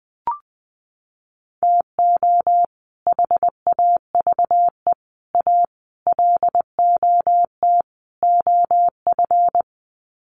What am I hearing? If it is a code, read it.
Morse code, heard as a keyed tone, about 20 wpm, 700 Hz. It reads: TO HAVE A LOT OF
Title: Sets of 5 Words Encore 20wpm